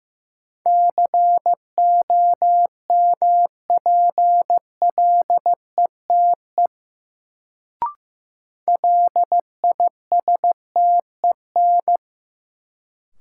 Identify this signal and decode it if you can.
Morse code, heard as a keyed tone, about 15 wpm, 700 Hz. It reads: COMPLETE LISTEN
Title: Sentences from Top 400 Words - New Word Drill - 15wpm